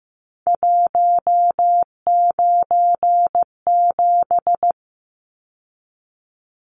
Morse code, heard as a keyed tone, about 15 wpm, 700 Hz. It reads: 197
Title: Numbers - 3-Digit 15wpm